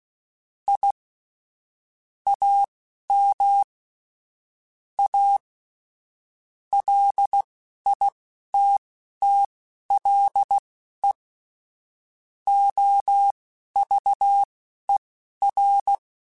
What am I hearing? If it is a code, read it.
Morse code, heard as a keyed tone, about 16 wpm, spaced as for 11 wpm, 800 Hz. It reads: I AM A LITTLE OVER